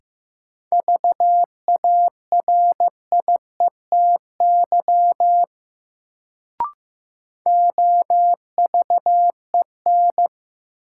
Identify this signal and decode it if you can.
Morse code, heard as a keyed tone, about 15 wpm, 700 Hz. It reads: VARIETY OVEN
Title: Sentences from Top 1000 Words - Review Part 4 - New Word Drill - 15wpm